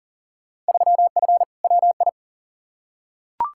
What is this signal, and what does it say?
Morse code, heard as a keyed tone, about 40 wpm, 700 Hz. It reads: 3F WI
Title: Field Day Exchanges - 40wpm